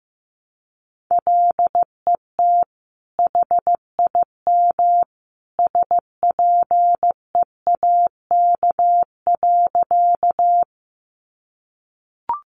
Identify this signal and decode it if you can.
Morse code, heard as a keyed tone, about 15 wpm, 700 Hz. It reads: LET HIM SPEAK.
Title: Sentences from Top 1100 Words 15wpm